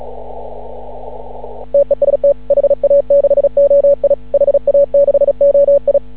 Morse code, heard as a keyed tone, about 36 wpm, 560 Hz. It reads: TEST HA6OI HA6OI
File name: CW / ha6oi 2